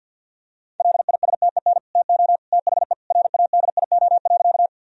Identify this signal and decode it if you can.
Morse code, heard as a keyed tone, about 50 wpm, 700 Hz. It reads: LISTEN TO THE RADIO.
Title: Sentences from Top 1000 Words - Review Part 3 - 50wpm